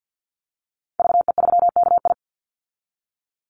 Morse code, heard as a keyed tone, about 52 wpm, 700 Hz. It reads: VE3KI
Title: Call Signs Repeated 2x - 50wpm